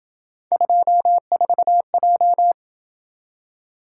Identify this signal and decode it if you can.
Morse code, heard as a keyed tone, about 27 wpm, 700 Hz. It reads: 24J